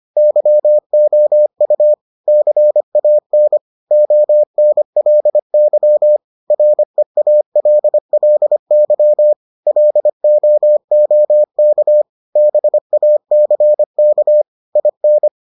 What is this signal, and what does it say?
Morse code, heard as a keyed tone, about 25 wpm, 600 Hz. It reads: YOU CAN ONLY REALLY LOOK BACK IN